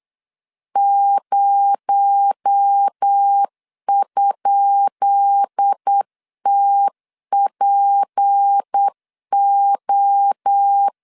Morse code, heard as a keyed tone, about 8 wpm, 800 Hz. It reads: 0?TPO
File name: LEZIONI / 10 SIGNS / 10B INT 8wpm 3534